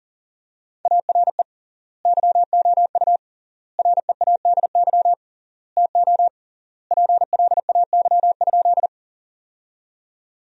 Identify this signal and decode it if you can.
Morse code, heard as a keyed tone, about 40 wpm, 700 Hz. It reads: ARE YOU READY TO PLAY?
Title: 3x Word-Spacing - Sentences from Top 400 Words - 40wpm